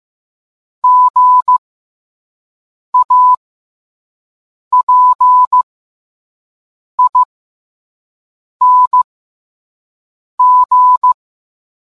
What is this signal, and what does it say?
Morse code, heard as a keyed tone, about 15 wpm, spaced as for 5 wpm, 1000 Hz. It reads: GAPING